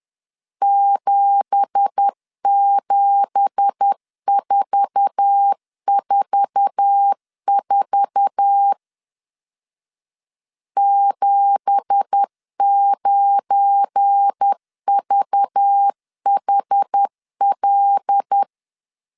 Morse code, heard as a keyed tone, about 10 wpm, 800 Hz. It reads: 77444 79VHL